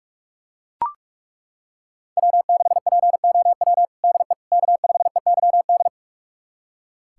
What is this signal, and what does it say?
Morse code, heard as a keyed tone, about 41 wpm, 700 Hz. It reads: W6POW DE K5EYD
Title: USA Callsign DE Callsign 45wpm